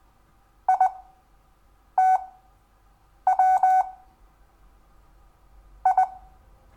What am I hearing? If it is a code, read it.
Morse code, heard as a keyed tone, about 20 wpm, spaced as for 7 wpm, 750 Hz. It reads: ITW I
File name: cw / Lesson6 WI